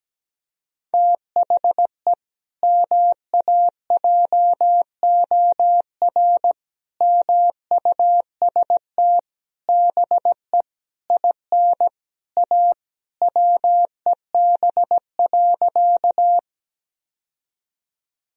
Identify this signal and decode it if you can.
Morse code, heard as a keyed tone, about 17 wpm, 700 Hz. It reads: THE MAJOR MUST BE IN A WEB.
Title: Sentences from Top 1000 Words - Review Part 4 - 17wpm